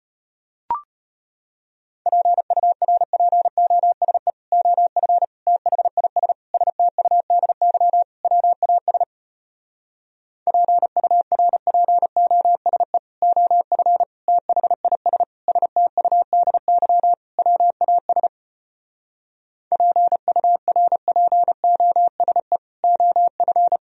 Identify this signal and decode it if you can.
Morse code, heard as a keyed tone, about 33 wpm, 700 Hz. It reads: PURPOSE OF THIS STUDY WAS PURPOSE OF THIS STUDY WAS PURPOSE OF